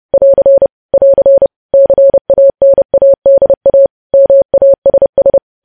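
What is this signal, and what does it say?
Morse code, heard as a keyed tone, about 30 wpm, 550 Hz. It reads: <AR> <AR> CANADA MASS
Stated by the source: News headlines sent in Morse code at 30.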